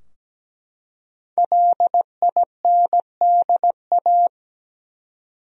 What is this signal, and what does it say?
Morse code, heard as a keyed tone, about 17 wpm, 700 Hz. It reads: LINDA
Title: USA Common Names - 17wpm